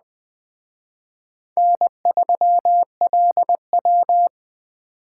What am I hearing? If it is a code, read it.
Morse code, heard as a keyed tone, about 21 wpm, 700 Hz. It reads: N3LW
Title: Speed Racing - USA 1x2 Call Signs 17wpm